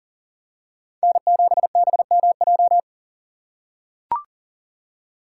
Morse code, heard as a keyed tone, about 37 wpm, 700 Hz. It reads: N7BMJ
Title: USA Call Signs - 5 Characters - 3x Character Spacing - 40wpm